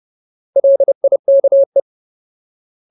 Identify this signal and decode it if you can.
Morse code, heard as a keyed tone, about 30 wpm, 550 Hz. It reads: LIKE
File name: Ultimate Word List / 30WPM Ultimate